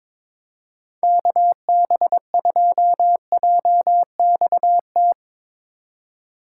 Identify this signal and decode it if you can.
Morse code, heard as a keyed tone, about 22 wpm, 700 Hz. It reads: KB2JXT
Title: USA Call Signs - 6 Characters - 4x Character Spacing - 22wpm